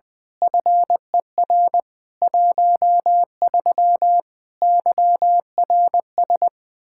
Morse code, heard as a keyed tone, about 20 wpm, 700 Hz. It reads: FER 13 YRS
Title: Practice QSO - Exchange 3 - More Info - 20wpm